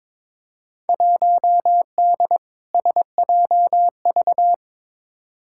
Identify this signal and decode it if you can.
Morse code, heard as a keyed tone, about 22 wpm, 700 Hz. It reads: 1D SJV
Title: Field Day Exchanges Encore - 22wpm